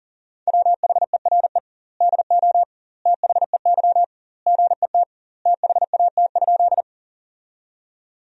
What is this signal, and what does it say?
Morse code, heard as a keyed tone, about 40 wpm, 700 Hz. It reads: WHERE DO THEY GET THAT?
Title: 2x Word-Spacing - Sentences from Top 200 Words - 40wpm